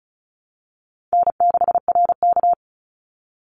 Morse code, heard as a keyed tone, about 36 wpm, 700 Hz. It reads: N6RK
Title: Call Signs - 35wpm